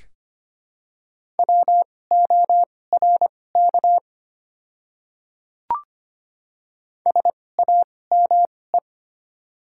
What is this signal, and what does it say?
Morse code, heard as a keyed tone, about 25 wpm, spaced as for 18 wpm, 690 Hz. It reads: WORK SAME
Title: 2x Character Spacing - Top 200 Words - 25wpm